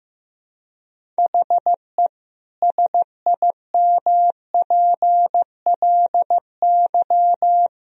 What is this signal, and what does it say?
Morse code, heard as a keyed tone, about 15 wpm, 700 Hz. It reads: HE SIMPLY
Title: Sentences from Top 1000 Words - Review Part 4 - 15wpm